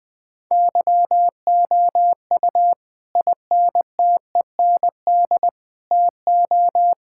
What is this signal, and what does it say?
Morse code, heard as a keyed tone, about 20 wpm, 700 Hz. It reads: YOU INTEND TO
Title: Sentences from Top 1600 Words 20wpm